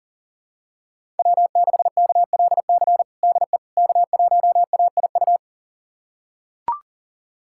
Morse code, heard as a keyed tone, about 40 wpm, 700 Hz. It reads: W6KLC DE K1AIU
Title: USA Callsign DE Callsign - 5 Characters - 40wpm